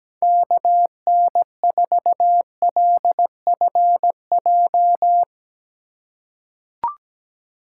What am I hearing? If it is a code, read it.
Morse code, heard as a keyed tone, about 17 wpm, 700 Hz. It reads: KN4LFJ
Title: USA Call Signs - 6 Characters - 3x Character Spacing - 17wpm